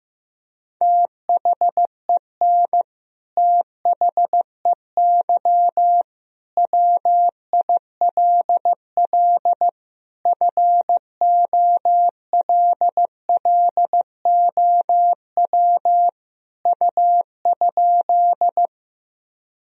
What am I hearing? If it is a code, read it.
Morse code, heard as a keyed tone, about 15 wpm, 700 Hz. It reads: THEN THEY WILL FOLLOW U?
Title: Sentences from Top 200 Words and Abbreviations 15wpm